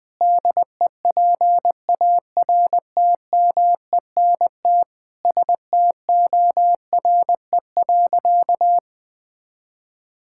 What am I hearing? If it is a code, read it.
Morse code, heard as a keyed tone, about 21 wpm, 700 Hz. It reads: DEPARTMENT STORE.
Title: Sentences from Top 1000 Words - Review Part 5 - 20wpm